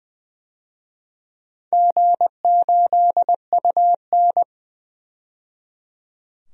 Morse code, heard as a keyed tone, about 20 wpm, 700 Hz. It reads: G8UN
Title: DXCC-100 Call Signs - 4 Characters - 20wpm